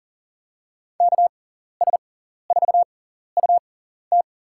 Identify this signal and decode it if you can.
Morse code, heard as a keyed tone, about 40 wpm, spaced as for 14 wpm, 700 Hz. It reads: KS4UT
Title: USA Call Signs - 5 Characters - 6x Character Spacing - 40wpm